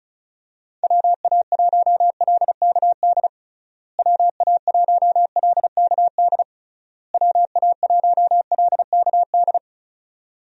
Morse code, heard as a keyed tone, about 36 wpm, 700 Hz. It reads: WA1LKD WA1LKD WA1LKD
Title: USA Call Signs - 6 Characters - Repeated 3x - 35wpm